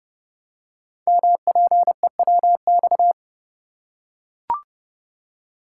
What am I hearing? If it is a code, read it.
Morse code, heard as a keyed tone, about 30 wpm, 700 Hz. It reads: MPEWX
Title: Code Groups - 5 Letters - 30wpm